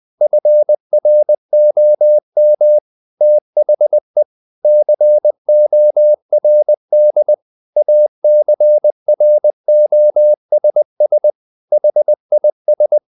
Morse code, heard as a keyed tone, about 20 wpm, 600 Hz. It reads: FROM THE CORD ACROSS HIS
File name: the old man and the sea part 2 20wpm